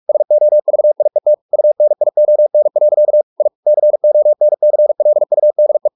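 Morse code, heard as a keyed tone, about 44 wpm, 600 Hz. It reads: SOVIET UNION. I CONCLUDE
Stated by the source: Morse practice at 45 wpm, the novel Never by Ken Follett in English, from CW med Gnister.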